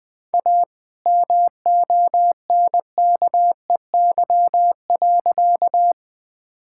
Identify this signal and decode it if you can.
Morse code, heard as a keyed tone, about 21 wpm, 700 Hz. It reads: A MONKEY.
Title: All-Age Appropriate Jokes - 20wpm